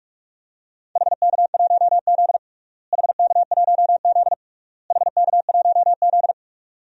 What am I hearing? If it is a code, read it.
Morse code, heard as a keyed tone, about 46 wpm, 700 Hz. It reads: HK1Z HK1Z HK1Z
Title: DXCC-100 Call Signs - 4 Characters - Repeated 3x - 45wpm